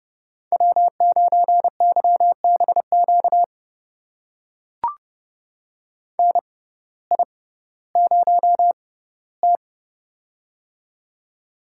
Morse code, heard as a keyed tone, about 30 wpm, spaced as for 17 wpm, 700 Hz. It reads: W9YBQ NI0T
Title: USA Call Signs - 6x Character Spacing - 30wpm